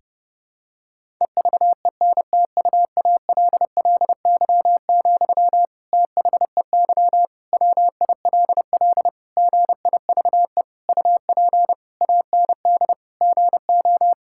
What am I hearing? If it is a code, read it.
Morse code, heard as a keyed tone, about 32 wpm, 700 Hz. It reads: EVENTUALLY, THEY WILL GIVE UP AND GO
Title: Sentences from Top 1000 Words - Review Part 6 - 30wpm